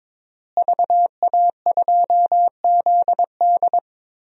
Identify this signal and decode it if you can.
Morse code, heard as a keyed tone, about 22 wpm, 700 Hz. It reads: VA2ZD
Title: DXCC-100 Call Signs - 5 Characters - Repeated 3x - 22wpm